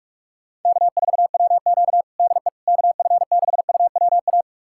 Morse code, heard as a keyed tone, about 46 wpm, 700 Hz. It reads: K4WQ DE KF6UWA
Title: USA Callsign DE Callsign 45wpm